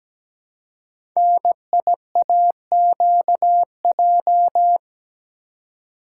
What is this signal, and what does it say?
Morse code, heard as a keyed tone, about 17 wpm, 700 Hz. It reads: NIAQJ
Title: Code Groups - 5 Letters - 17wpm